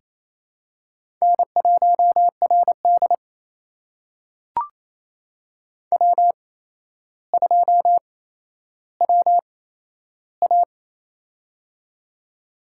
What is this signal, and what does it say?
Morse code, heard as a keyed tone, about 28 wpm, spaced as for 12 wpm, 700 Hz. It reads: N1RD W2WA
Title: USA Call Signs - 4 Characters - 8x Character Spacing - 28wpm